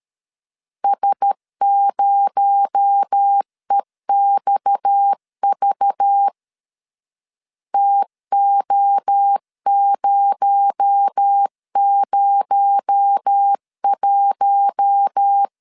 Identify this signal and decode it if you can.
Morse code, heard as a keyed tone, about 13 wpm, 800 Hz. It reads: S0EXV TO001
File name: LEZIONI / 6 PKX0 / 6 PKX0 12wpm 2231